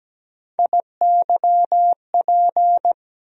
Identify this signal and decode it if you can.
Morse code, heard as a keyed tone, about 17 wpm, 700 Hz. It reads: IYP
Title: USA Call Sign Postfixes - 3 Letters 17wpm